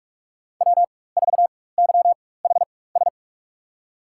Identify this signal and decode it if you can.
Morse code, heard as a keyed tone, about 46 wpm, spaced as for 21 wpm, 700 Hz. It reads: W4YHS
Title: USA Call Signs - 5 Characters - 4x Character Spacing - 45wpm